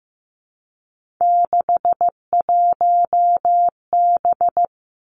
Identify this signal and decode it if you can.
Morse code, heard as a keyed tone, about 15 wpm, 700 Hz. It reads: 61B
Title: Three Challenging Letter-Number Combinations 15wpm